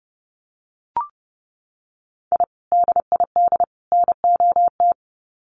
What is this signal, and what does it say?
Morse code, heard as a keyed tone, about 30 wpm, 700 Hz. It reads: I DID NOT
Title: Sets of 3 Words 30wpm